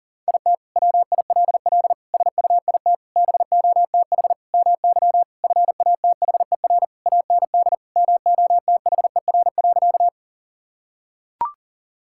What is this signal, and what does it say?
Morse code, heard as a keyed tone, about 40 wpm, 700 Hz. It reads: IT WILL SUIT BOTH MY FATHER AND MOTHER.
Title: Sentences from Top 1600 Words 40wpm